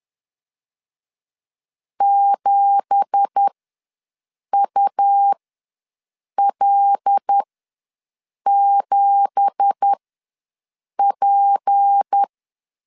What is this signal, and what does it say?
Morse code, heard as a keyed tone, about 10 wpm, spaced as for 6 wpm, 800 Hz. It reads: 7UL7P
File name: LEZIONI / 9 2867 ws 10wpm 4215